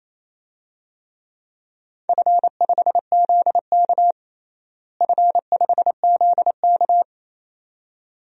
Morse code, heard as a keyed tone, about 28 wpm, 700 Hz. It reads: F5ZK F5ZK
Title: DXCC-100 Call Signs - 4 Characters - Repeated 2x - 28wpm